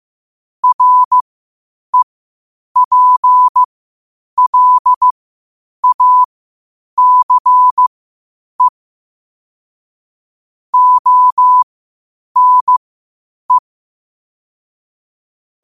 Morse code, heard as a keyed tone, about 15 wpm, spaced as for 8 wpm, 1000 Hz. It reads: REPLACE ONE
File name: qotdfiles / 8wpm